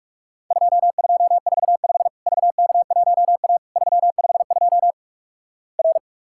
Morse code, heard as a keyed tone, about 46 wpm, 700 Hz. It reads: 2245 VK1A 352 R
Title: Worked All Europe DX Contest - QTC Exchange - 45wpm